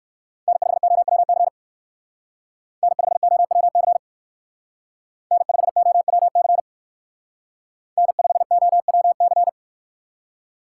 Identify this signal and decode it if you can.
Morse code, heard as a keyed tone, about 62 wpm, 700 Hz. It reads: N5OWC N5OWC N5OWC N5OWC